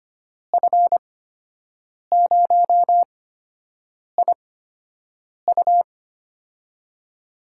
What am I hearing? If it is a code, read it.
Morse code, heard as a keyed tone, about 25 wpm, spaced as for 7 wpm, 700 Hz. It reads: F0IU
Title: USA Call Signs - 5 Characters - 8x Character Spacing - 25wpm